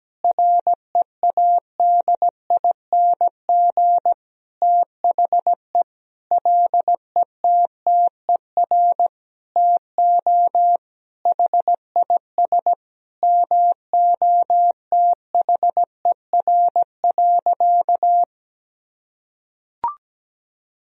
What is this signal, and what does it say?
Morse code, heard as a keyed tone, about 17 wpm, 700 Hz. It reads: READING THE LETTER TO HIS MOTHER.
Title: Sentences from Top 1000 Words - Review Part 3 - 17wpm